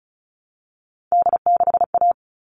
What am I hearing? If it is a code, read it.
Morse code, heard as a keyed tone, about 34 wpm, 700 Hz. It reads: D6A
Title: All Cut Numbers - 3-Digit 35wpm